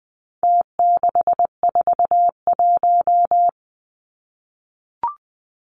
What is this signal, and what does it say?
Morse code, heard as a keyed tone, about 20 wpm, 700 Hz. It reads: T641
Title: Numbers - Contest Style - 4-Digit 20wpm